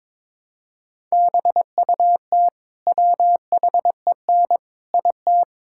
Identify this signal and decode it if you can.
Morse code, heard as a keyed tone, about 22 wpm, 700 Hz. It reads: BUT WHEN IT
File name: practice / 22wpm / Build A Phrase - 5-Word Phrases - 22wpm